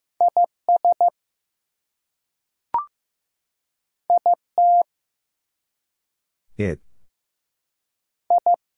Morse code, heard as a keyed tone, about 15 wpm, 700 Hz. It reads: IS IT I
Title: Two Letter Words - 15wpm